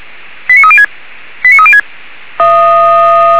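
Five-tone selective call (ZVEI), heard as quick runs of five tones, tones of about 70 ms. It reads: 89306 89306
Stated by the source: ZVEI five-tone call received on SDR.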